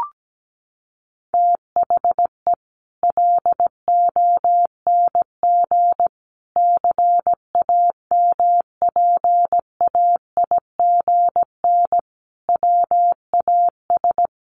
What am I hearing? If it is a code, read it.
Morse code, heard as a keyed tone, about 17 wpm, 700 Hz. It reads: THE LONG CAMPAIGN WAS
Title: Sentences from Top 1200 Words 17wpm